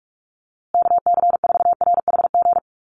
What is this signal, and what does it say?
Morse code, heard as a keyed tone, about 44 wpm, 700 Hz. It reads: KC4RHG
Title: USA Call Signs Repeated 2x 45wpm